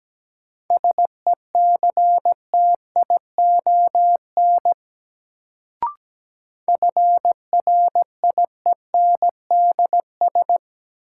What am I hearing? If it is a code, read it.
Morse code, heard as a keyed tone, about 17 wpm, 700 Hz. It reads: SECTION FRIENDS
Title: Sentences from Top 1000 Words - Review Part 5 - New Word Drill - 17wpm